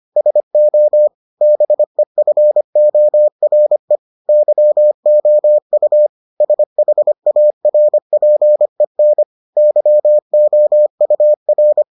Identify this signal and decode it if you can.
Morse code, heard as a keyed tone, about 25 wpm, 600 Hz. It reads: SO BEFORE YOU SHARPEN YOUR